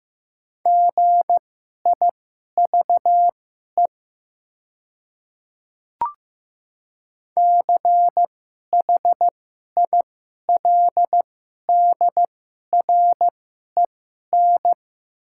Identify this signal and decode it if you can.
Morse code, heard as a keyed tone, about 15 wpm, spaced as for 11 wpm, 700 Hz. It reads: GIVE CHILDREN
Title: 2x Character Spacing - Top 300 Words - 15wpm